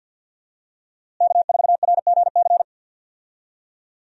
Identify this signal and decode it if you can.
Morse code, heard as a keyed tone, about 51 wpm, 700 Hz. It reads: K4RGC
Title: USA Call Signs - 5 Characters - Repeated 2x - 50wpm